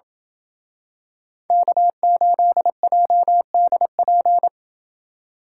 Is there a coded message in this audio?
Yes: Morse code from 1.5 to 4.5 s, heard as a keyed tone, about 27 wpm, 700 Hz. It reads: K8JDP